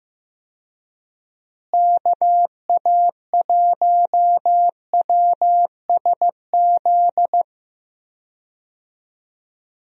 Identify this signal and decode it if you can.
Morse code, heard as a keyed tone, about 15 wpm, 700 Hz. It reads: KA1WSZ
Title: USA Call Signs - 6 Characters - Repeated 2x - 15wpm